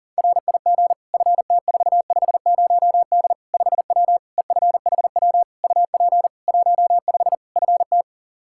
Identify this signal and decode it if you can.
Morse code, heard as a keyed tone, about 37 wpm, 700 Hz. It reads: RIG FT450D 5W EFHW UP 15 FT
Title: Practice QSO - Exchange 2 - Info - 40wpm